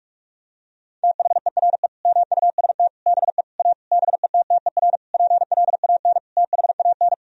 Morse code, heard as a keyed tone, about 46 wpm, 700 Hz. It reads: THERE MUST BE A BETTER PLAN THAN
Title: Sentences from Top 1000 Words - Review Part 4 - 45wpm